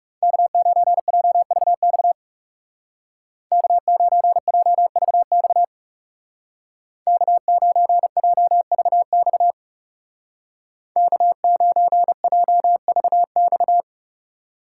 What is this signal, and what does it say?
Morse code, heard as a keyed tone, about 38 wpm, 700 Hz. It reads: K9JVX K9JVX K9JVX K9JVX